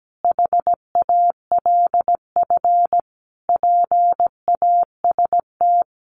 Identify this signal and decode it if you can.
Morse code, heard as a keyed tone, about 17 wpm, 700 Hz. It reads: HALF PAST
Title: Sentences from Top 1300 Words 17wpm